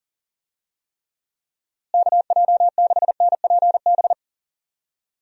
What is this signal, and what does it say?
Morse code, heard as a keyed tone, about 37 wpm, 700 Hz. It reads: KJ6NPB
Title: USA Call Signs - 6 Characters - Repeated 3x - 40wpm